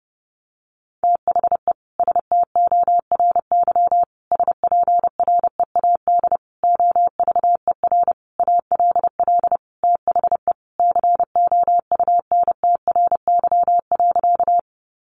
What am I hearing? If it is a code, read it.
Morse code, heard as a keyed tone, about 30 wpm, 700 Hz. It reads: THE STORY SPREAD OVER ALL THE COUNTRY.
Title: Sentences from Top 1200 Words 30wpm